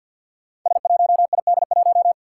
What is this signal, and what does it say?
Morse code, heard as a keyed tone, about 49 wpm, 700 Hz. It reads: S1ID1
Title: Five Challenging Letter-Number Combinations 50wpm